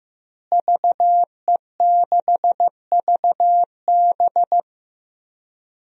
Morse code, heard as a keyed tone, about 15 wpm, 700 Hz. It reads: VE6VB
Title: DXCC-100 Call Signs - 5 Characters - Repeated 3x - 15wpm